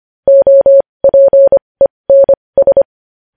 Morse code, heard as a keyed tone, about 25 wpm, spaced as for 20 wpm, 550 Hz. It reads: OPENS
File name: podcast / News headlines-1x-25x20